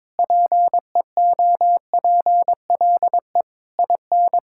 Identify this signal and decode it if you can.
Morse code, heard as a keyed tone, about 21 wpm, 700 Hz. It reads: PEOPLE IN
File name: practice / 22wpm / Sentences with 3-Word Phrases - 22wpm